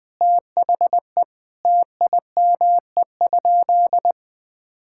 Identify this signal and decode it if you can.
Morse code, heard as a keyed tone, about 20 wpm, 700 Hz. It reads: THE TIME?
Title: Sentences with 2-Word Phrases - Spoken Beginning - 20wpm